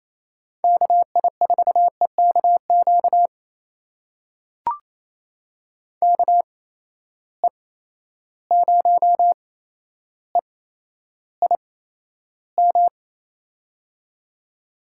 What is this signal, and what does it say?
Morse code, heard as a keyed tone, about 28 wpm, spaced as for 12 wpm, 700 Hz. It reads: KI4EKQ KE0EIM
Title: USA Call Signs - 6 Characters - 8x Character Spacing - 28wpm